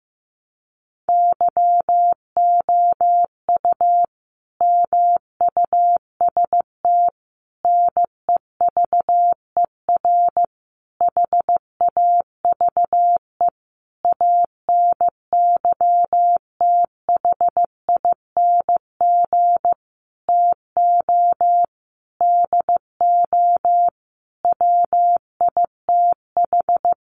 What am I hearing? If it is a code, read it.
Morse code, heard as a keyed tone, about 15 wpm, 700 Hz. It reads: YOU MUST NEVER HAVE ANYTHING TO DO WITH
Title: Sentences from Top 1000 Words 15wpm